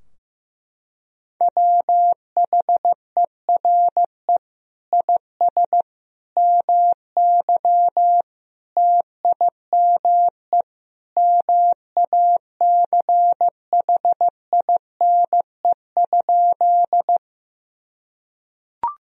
Morse code, heard as a keyed tone, about 15 wpm, 700 Hz. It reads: WHERE IS MY TIME MACHINE?
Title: Sentences from Top 1000 Words - Review Part 2 - 15wpm